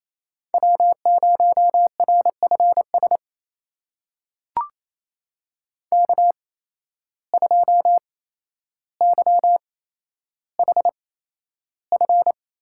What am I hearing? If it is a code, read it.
Morse code, heard as a keyed tone, about 28 wpm, spaced as for 12 wpm, 700 Hz. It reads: W0RFS K2YHF
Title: USA Call Signs - 5 Characters - 8x Character Spacing - 28wpm